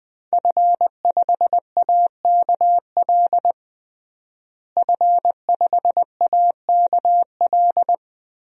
Morse code, heard as a keyed tone, about 20 wpm, 700 Hz. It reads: F5AKL F5AKL
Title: DXCC-100 Call Signs - 5 Characters - Repeated 2x - 20wpm